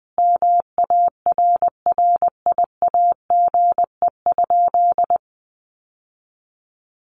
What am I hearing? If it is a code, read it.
Morse code, heard as a keyed tone, about 20 wpm, 700 Hz. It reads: MARRIAGE?
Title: Sentences from Top 1200 Words 20wpm